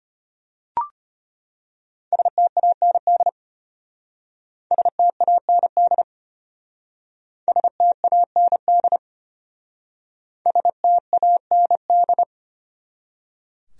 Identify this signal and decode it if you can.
Morse code, heard as a keyed tone, about 31 wpm, 700 Hz. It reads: STAND STAND STAND STAND
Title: Speed-Racing Top 300 Words 25wpm